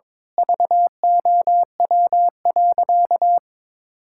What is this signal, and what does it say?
Morse code, heard as a keyed tone, about 22 wpm, 700 Hz. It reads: VOW.
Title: Pangram Challenge v2 - 22wpm